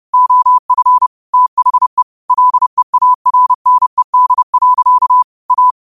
Morse code, heard as a keyed tone, about 30 wpm, 1000 Hz. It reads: OF THE LEARNED. A
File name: qotdfiles / 30wpm